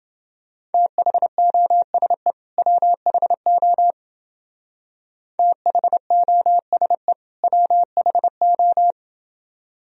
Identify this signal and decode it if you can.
Morse code, heard as a keyed tone, about 29 wpm, 700 Hz. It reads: THOSE WHO THOSE WHO